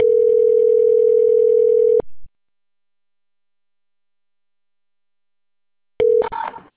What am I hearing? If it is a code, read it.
Ringback tone (North America), 440 + 480 Hz.